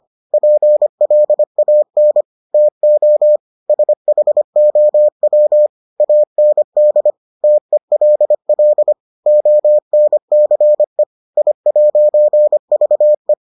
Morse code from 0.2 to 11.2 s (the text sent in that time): PLAN TO SHOW AND TELL ONCE